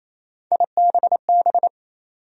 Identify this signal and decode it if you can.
Morse code, heard as a keyed tone, about 28 wpm, 700 Hz. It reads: IBB
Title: USA Call Sign Postfixes - 3 Letters 28wpm